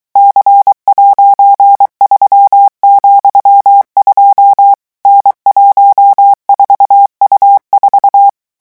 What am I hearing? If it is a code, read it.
Morse code, heard as a keyed tone, about 23 wpm, 800 Hz. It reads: C'3,2 N14U4
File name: morse mp3 / revision13-vitesse 22 mots